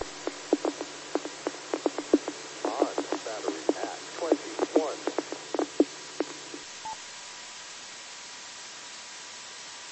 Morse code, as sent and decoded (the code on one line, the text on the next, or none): none